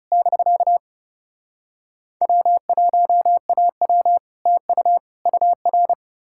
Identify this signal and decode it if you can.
Morse code, heard as a keyed tone, about 37 wpm, 700 Hz. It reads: <BK> W1AW TU UR
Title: Speed-Racing Common QSO Phrases Encore - 30wpm